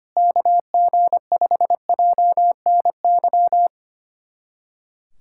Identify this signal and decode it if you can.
Morse code, heard as a keyed tone, about 25 wpm, 700 Hz. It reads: KG5JNY
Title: USA Callsign DE Callsign - 6 Characters - 25wpm